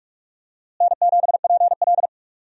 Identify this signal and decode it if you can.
Morse code, heard as a keyed tone, about 46 wpm, 700 Hz. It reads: N7PL
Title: USA Call Signs - 4 Characters - Repeated 3x - 45wpm